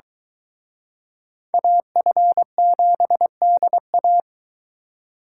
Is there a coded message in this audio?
Yes: Morse code from 1.5 to 4.2 s, heard as a keyed tone, about 23 wpm, 700 Hz. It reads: AF7DA